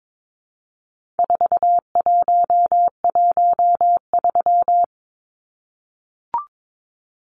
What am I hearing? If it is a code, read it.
Morse code, heard as a keyed tone, about 22 wpm, 700 Hz. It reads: 4113
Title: Numbers - 4-Digit 22wpm